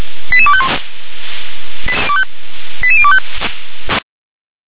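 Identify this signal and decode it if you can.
Five-tone selective call (ZVEI), heard as quick runs of five tones, tones of about 70 ms. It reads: 88251 88025
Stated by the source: ZVEI five-tone selective call received on an SDR.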